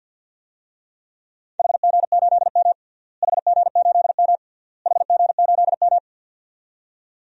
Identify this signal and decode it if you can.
Morse code, heard as a keyed tone, about 53 wpm, 700 Hz. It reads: HG8M HG8M HG8M
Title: DXCC-100 Call Signs - 4 Characters - Repeated 3x - 50wpm